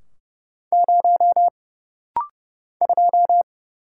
Morse code, heard as a keyed tone, about 30 wpm, 700 Hz. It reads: 0 2
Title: Single Letter-Number - Rapid-Fire 30wpm